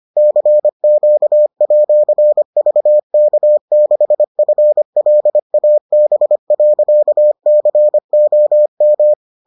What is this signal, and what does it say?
Morse code, heard as a keyed tone, about 25 wpm, 600 Hz. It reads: CQ@VK6FLAB.COM